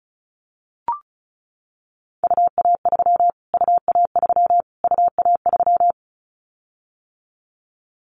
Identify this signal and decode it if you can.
Morse code, heard as a keyed tone, about 34 wpm, 700 Hz. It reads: UA3 UA3 UA3
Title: Callsign Prefixes for Top 25 DXCC - Repeated 3x - 35wpm